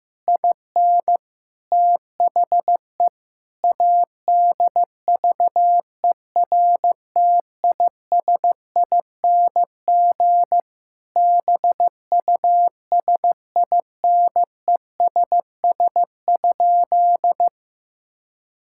Morse code, heard as a keyed tone, about 15 wpm, 700 Hz. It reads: IN THE ADVERTISING BUSINESS?
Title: Sentences from Top 1000 Words - Review Part 6 - 15wpm